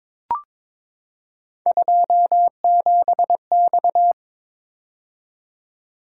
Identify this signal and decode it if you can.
Morse code, heard as a keyed tone, about 22 wpm, 700 Hz. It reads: 27X
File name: practice / 22wpm / Three Letter-Number Combinations - 22wpm